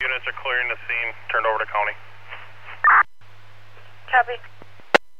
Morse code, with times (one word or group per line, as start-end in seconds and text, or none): none